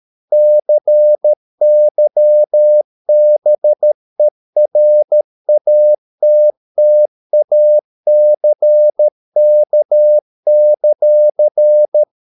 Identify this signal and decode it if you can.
Morse code, heard as a keyed tone, about 13 wpm, 600 Hz. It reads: CYBERATTACK;